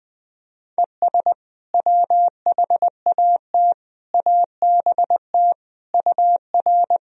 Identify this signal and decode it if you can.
Morse code, heard as a keyed tone, about 20 wpm, 700 Hz. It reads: ES WHAT ABT UR
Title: Sentences from Top 200 Words and Abbreviations 20wpm